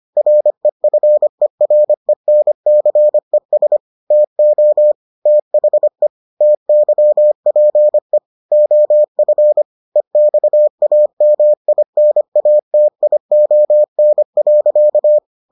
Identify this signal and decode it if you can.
Morse code, heard as a keyed tone, about 25 wpm, 600 Hz. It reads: REFERENCES TO THE TYPE OF EXAMINATION.